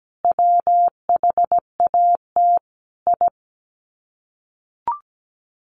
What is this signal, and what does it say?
Morse code, heard as a keyed tone, about 17 wpm, 700 Hz. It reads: WHAT I
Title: Sets of 4 Words 17wpm